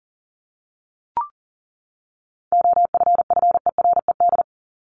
Morse code, heard as a keyed tone, about 40 wpm, 700 Hz. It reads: OFFERED
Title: ED Suffix - Top 100 Words with Base Words 40wpm